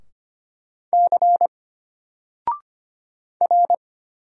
Morse code, heard as a keyed tone, about 25 wpm, 700 Hz. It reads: C R
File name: Single-Letter-Number-25wpm